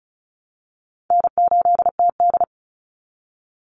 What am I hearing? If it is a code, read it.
Morse code, heard as a keyed tone, about 34 wpm, 700 Hz. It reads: N8TD
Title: Easy Call Signs Repeated 3x - 35wpm